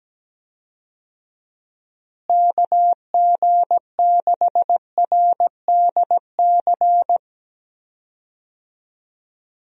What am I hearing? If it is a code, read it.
Morse code, heard as a keyed tone, about 17 wpm, 700 Hz. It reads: KG6RDC
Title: USA Call Signs - 6 Characters - Repeated 2x - 17wpm